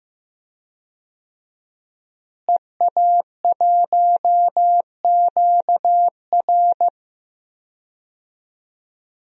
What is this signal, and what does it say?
Morse code, heard as a keyed tone, about 16 wpm, 700 Hz. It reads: EA1QR
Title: DXCC-100 Call Signs - 5 Characters - Repeated 2x - 15wpm